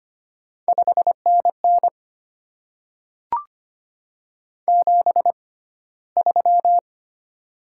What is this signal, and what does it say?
Morse code, heard as a keyed tone, about 25 wpm, spaced as for 15 wpm, 700 Hz. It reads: 5NN 73
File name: Top 20 QSO Elements - 6x Character Spacing - 25wpm